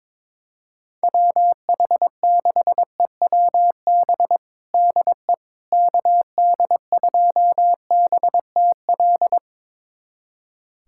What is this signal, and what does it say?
Morse code, heard as a keyed tone, about 22 wpm, 700 Hz. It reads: WH6EWB DE KD2BTL
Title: USA Callsign DE Callsign - 6 Characters - 22wpm